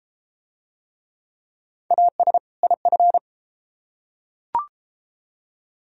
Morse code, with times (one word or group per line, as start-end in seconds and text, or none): 1.9-2.4 AS
2.6-3.2 IF